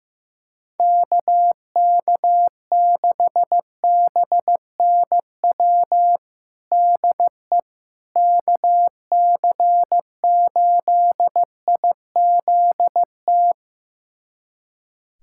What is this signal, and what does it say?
Morse code, heard as a keyed tone, about 15 wpm, 700 Hz. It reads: KK6BNW DE KC8IZT
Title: USA Callsign DE Callsign 15wpm